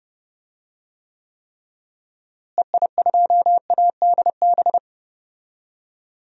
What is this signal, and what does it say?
Morse code, heard as a keyed tone, about 32 wpm, 700 Hz. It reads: EI2ADB
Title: DXCC-100 Call Signs - 6 Characters - Repeated 3x - 30wpm